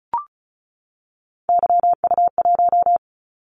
Morse code, heard as a keyed tone, about 36 wpm, 700 Hz. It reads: YU1
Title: Callsign Prefixes for Top 25 DXCC - 35wpm